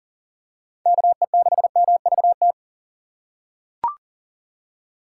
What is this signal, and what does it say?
Morse code, heard as a keyed tone, about 39 wpm, 700 Hz. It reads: KE6MVT
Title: USA Call Signs - 6 Characters - 2x Character Spacing - 40wpm